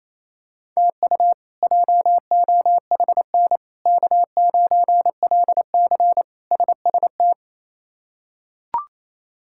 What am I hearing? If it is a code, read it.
Morse code, heard as a keyed tone, about 28 wpm, 700 Hz. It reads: TU JOHN K9LC SST
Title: K1USN SST QSO Practice - 28wpm